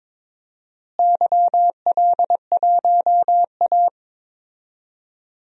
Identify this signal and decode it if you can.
Morse code, heard as a keyed tone, about 22 wpm, 700 Hz. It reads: YL1A
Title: DXCC-100 Call Signs - 4 Characters - Repeated 3x - 22wpm